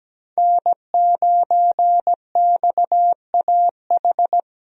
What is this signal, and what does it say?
Morse code, heard as a keyed tone, about 17 wpm, 700 Hz. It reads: N9XAH
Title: USA Call Signs - 5 Characters - Repeated 2x - 17wpm